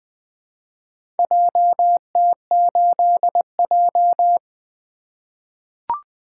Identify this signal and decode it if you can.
Morse code, heard as a keyed tone, about 20 wpm, 700 Hz. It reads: JT8J
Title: Four Letter-Number Combinations - 20wpm